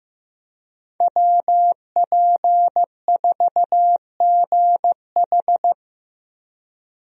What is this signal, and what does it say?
Morse code, heard as a keyed tone, about 15 wpm, 700 Hz. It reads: WP4GH
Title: USA Call Signs - 5 Characters - 15wpm